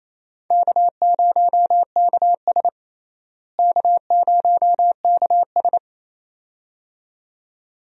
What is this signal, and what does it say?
Morse code, heard as a keyed tone, about 27 wpm, 700 Hz. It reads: K0KS K0KS
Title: USA Call Signs - 4 Characters - Repeated 2x - 28wpm